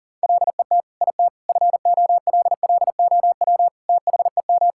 Morse code, heard as a keyed tone, about 40 wpm, 700 Hz. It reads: LET IT FOLLOW THEM
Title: Sentences from Top 1600 Words 40wpm